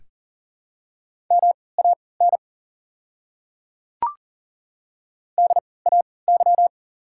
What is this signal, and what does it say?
Morse code, heard as a keyed tone, about 38 wpm, spaced as for 22 wpm, 690 Hz. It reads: MAN DAY
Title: 3x Character Spacing - Top 200 Words - 40wpm